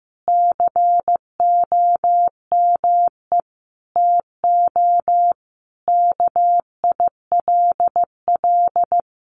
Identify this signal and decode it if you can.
Morse code, heard as a keyed tone, about 15 wpm, 700 Hz. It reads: COME TO KILL
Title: Sentences from Top 1400 Words 15wpm